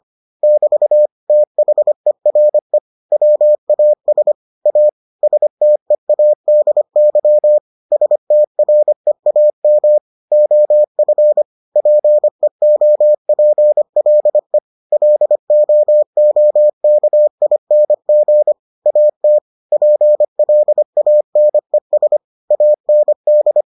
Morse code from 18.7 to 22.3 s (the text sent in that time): AT PLANES